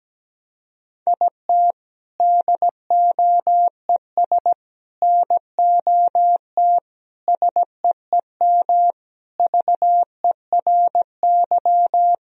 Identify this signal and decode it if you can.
Morse code, heard as a keyed tone, about 17 wpm, 700 Hz. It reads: IT DOES NOT SEEM VERY
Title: Sentences from Top 1000 Words - Review Part 1 - 17wpm